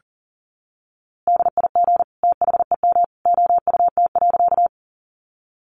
Morse code, heard as a keyed tone, about 40 wpm, 700 Hz. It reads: DIG THEM OUT.
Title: A Sound Of Thunder 40wpm